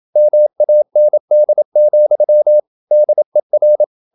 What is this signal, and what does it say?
Morse code, heard as a keyed tone, about 27 wpm, 600 Hz. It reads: MAND, DER
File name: Ferie QRQ 27wpm